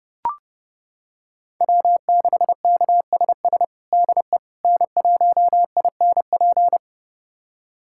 Morse code, heard as a keyed tone, about 30 wpm, 700 Hz. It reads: W6KSS DE N1INP
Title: USA Callsign DE Callsign - 5 Characters - 30wpm